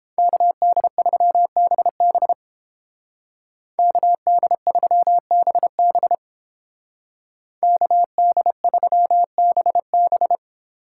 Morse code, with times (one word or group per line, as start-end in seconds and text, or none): 0.2-2.3 KD3BB
3.8-6.1 KD3BB
7.6-10.3 KD3BB